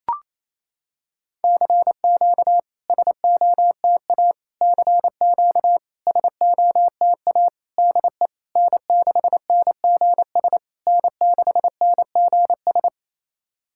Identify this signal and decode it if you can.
Morse code, heard as a keyed tone, about 27 wpm, 690 Hz. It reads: CQ SOTA CQ SOTA DE N6NGS N6NGS
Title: SOTA QSO Practice 28wpm